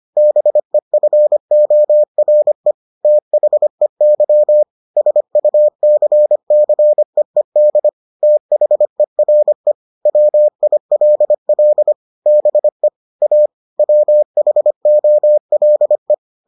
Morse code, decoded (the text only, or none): BEFORE THEY SUCCEED THERE WILL BE A WHOLE